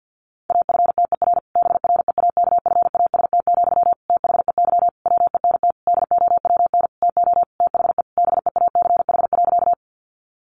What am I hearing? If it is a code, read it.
Morse code, heard as a keyed tone, about 50 wpm, 700 Hz. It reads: AFTER BREAKFAST, THEY WENT DOWN TO THE BEACH.
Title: Sentences from Top 1500 Words 50wpm